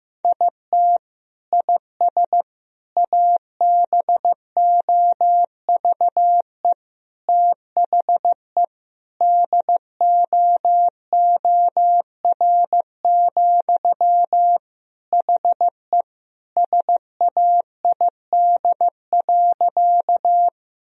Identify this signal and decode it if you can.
Morse code, heard as a keyed tone, about 15 wpm, 700 Hz. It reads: IT IS ABOVE THE DOOR, HE SAID.
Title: Sentences from Top 1000 Words - Review Part 2 - 15wpm